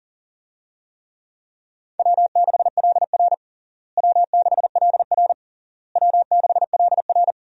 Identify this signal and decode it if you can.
Morse code, heard as a keyed tone, about 40 wpm, 700 Hz. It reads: W6LR W6LR W6LR
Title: USA Call Signs - 4 Characters - Repeated 3x - 40wpm